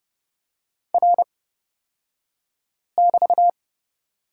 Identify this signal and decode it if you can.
Morse code, heard as a keyed tone, about 30 wpm, spaced as for 5 wpm, 700 Hz. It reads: R<BT>
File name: Lesson 37d - Letter x - Two Characters 30wpm